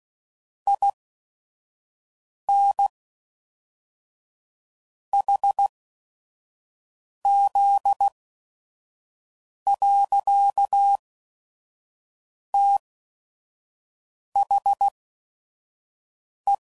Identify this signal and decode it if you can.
Morse code, heard as a keyed tone, about 16 wpm, spaced as for 5 wpm, 800 Hz. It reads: IN HZ. THE
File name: CW Code Groups @5WPM / Technical Subjects / XC